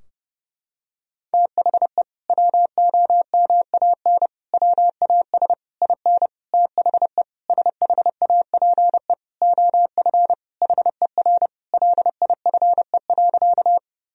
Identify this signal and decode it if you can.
Morse code, heard as a keyed tone, about 30 wpm, 700 Hz. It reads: THE WOMAN WAS IN THE SHAPE OF HER LIFE.
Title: Sentences from Top 1000 Words - Review Part 3 - 30wpm